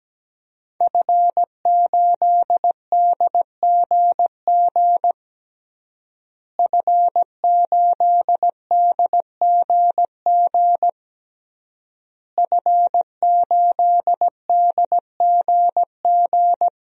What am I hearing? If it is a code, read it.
Morse code, heard as a keyed tone, about 17 wpm, 700 Hz. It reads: F8DGG F8DGG F8DGG
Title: DXCC-100 Call Signs - 5 Characters - Repeated 3x - 17wpm